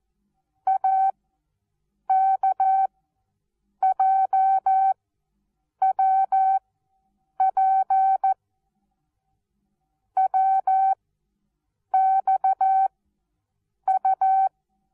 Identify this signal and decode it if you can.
Morse code, heard as a keyed tone, about 14 wpm, 770 Hz. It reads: A K J W P W X U
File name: Onlinelehrgang / morsen / DJ4UF Morsen19